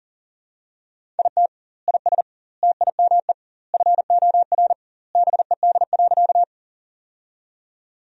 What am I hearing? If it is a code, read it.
Morse code, heard as a keyed tone, about 40 wpm, 700 Hz. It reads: IT IS TIME FOR BED.
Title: 2x Word-Spacing - Sentences from Top 500 Words - 40wpm